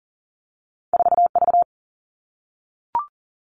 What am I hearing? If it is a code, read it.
Morse code, heard as a keyed tone, about 40 wpm, 700 Hz. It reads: 4V
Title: All Cut Numbers - 2-Digit 40wpm